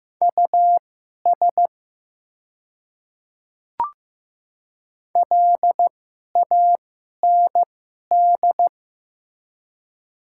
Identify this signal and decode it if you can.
Morse code, heard as a keyed tone, about 15 wpm, spaced as for 11 wpm, 700 Hz. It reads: US LAND
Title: 2x Character Spacing - Top 300 Words - 15wpm